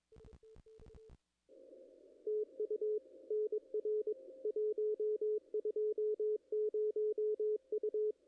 Morse code, heard as a keyed tone, about 20 wpm, 440 Hz. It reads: TU NR 120 U
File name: contests / ss-cw-k5tr-op-n5rz -0100